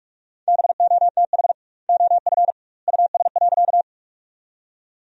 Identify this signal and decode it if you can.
Morse code, heard as a keyed tone, about 46 wpm, 700 Hz. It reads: BOTH OF US.
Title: 2x Word-Spacing - Sentences from Top 300 Words - 45wpm